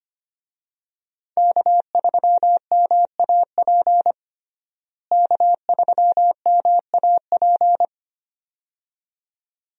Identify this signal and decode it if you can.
Morse code, heard as a keyed tone, about 25 wpm, 700 Hz. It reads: K3MAP K3MAP
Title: USA Call Signs - 5 Characters - Repeated 2x - 25wpm